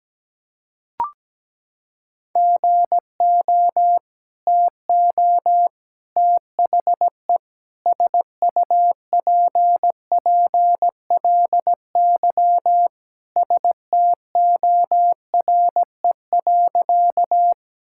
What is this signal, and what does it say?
Morse code, heard as a keyed tone, about 17 wpm, 700 Hz. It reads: GO TO THE SUPPLY STORE.
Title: Sentences from Top 1000 Words - Review Part 5 - 17wpm